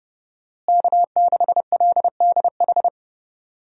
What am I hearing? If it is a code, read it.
Morse code, heard as a keyed tone, about 31 wpm, 700 Hz. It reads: K6LDH
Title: USA Call Signs - 5 Characters - Repeated 2x - 30wpm